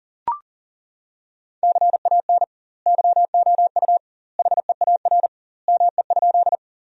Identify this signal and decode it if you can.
Morse code, heard as a keyed tone, about 37 wpm, 700 Hz. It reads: CAN YOU HEAR ME?
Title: 2x Word-Spacing - Sentences from Top 300 Words - 40wpm